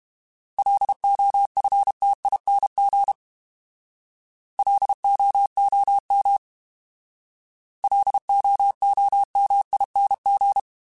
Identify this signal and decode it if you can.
Morse code, heard as a keyed tone, about 32 wpm, 800 Hz. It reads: LOFTING LOOM LOOMING